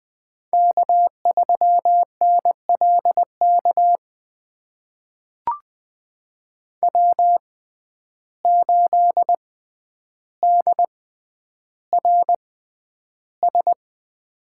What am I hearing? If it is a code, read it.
Morse code, heard as a keyed tone, about 20 wpm, spaced as for 10 wpm, 700 Hz. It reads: K3NLK W8DRS
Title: USA Call Signs - 5 Characters - 6x Character Spacing - 20wpm